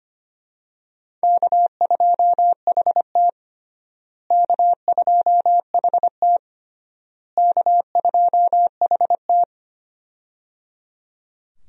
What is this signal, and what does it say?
Morse code, heard as a keyed tone, about 25 wpm, 700 Hz. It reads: K2HT K2HT K2HT
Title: USA Call Signs - 4 Characters - Repeated 3x - 25wpm